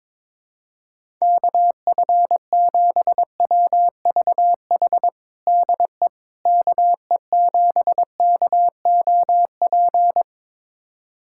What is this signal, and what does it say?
Morse code, heard as a keyed tone, about 22 wpm, 700 Hz. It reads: KF7WVH DE KE7KOP
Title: USA Callsign DE Callsign - 6 Characters - 22wpm